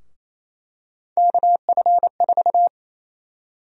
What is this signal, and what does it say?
Morse code, heard as a keyed tone, about 28 wpm, 700 Hz. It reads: KF4
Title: USA Call Sign Prefixes - 28wpm